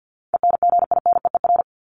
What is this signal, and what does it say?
Morse code, heard as a keyed tone, about 50 wpm, 700 Hz. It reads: ENGINEER
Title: ING Suffix - Top 100 Words with Base Words 50wpm